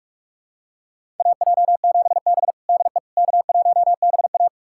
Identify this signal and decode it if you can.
Morse code, heard as a keyed tone, about 46 wpm, 700 Hz. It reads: AJ7B DE K1BA
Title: USA Callsign DE Callsign - 4 Characters - 45wpm